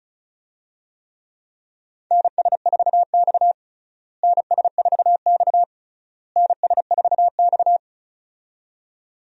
Morse code, heard as a keyed tone, about 34 wpm, 700 Hz. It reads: NS4X NS4X NS4X
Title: USA Call Signs - 4 Characters - Repeated 3x - 35wpm